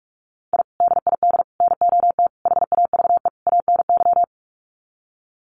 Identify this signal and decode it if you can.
Morse code, heard as a keyed tone, about 44 wpm, 700 Hz. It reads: I DID NOT HAVE ANY
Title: Sets of 5 Words Encore 45wpm